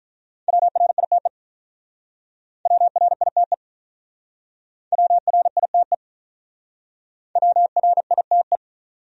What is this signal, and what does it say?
Morse code, heard as a keyed tone, about 44 wpm, 700 Hz. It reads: WRITE WRITE WRITE WRITE